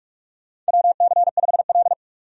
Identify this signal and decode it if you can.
Morse code, heard as a keyed tone, about 41 wpm, 700 Hz. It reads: WC5L
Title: USA Call Signs - 4 Characters - 45wpm